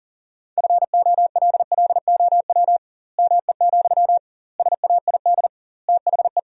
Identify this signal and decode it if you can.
Morse code, heard as a keyed tone, about 40 wpm, 700 Hz. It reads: FOLLOW ME, SAID THE
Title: 2x Word-Spacing - Sentences from Top 200 Words - 40wpm